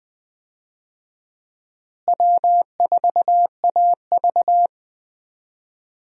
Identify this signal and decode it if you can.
Morse code, heard as a keyed tone, about 20 wpm, 700 Hz. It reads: W4AV
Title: USA Call Signs - 4 Characters - Repeated 3x - 20wpm